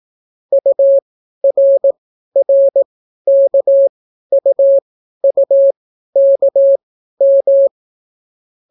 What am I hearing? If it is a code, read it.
Morse code, heard as a keyed tone, about 18 wpm, spaced as for 12 wpm, 550 Hz. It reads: URRKUUKM